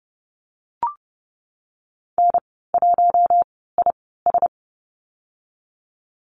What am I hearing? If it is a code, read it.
Morse code, heard as a keyed tone, about 30 wpm, spaced as for 17 wpm, 700 Hz. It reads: N1IS
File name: Lesson 19g - Letter w - Call Signs 30wpm